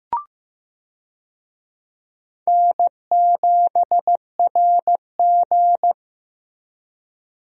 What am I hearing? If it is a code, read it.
Morse code, heard as a keyed tone, about 15 wpm, 700 Hz. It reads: N7RG
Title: USA Call Signs - 4 Characters - Repeated 2x - 15wpm